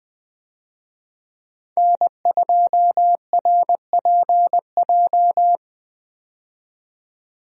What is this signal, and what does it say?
Morse code, heard as a keyed tone, about 20 wpm, 700 Hz. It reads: N2RPJ
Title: USA Call Signs - 5 Characters - 20wpm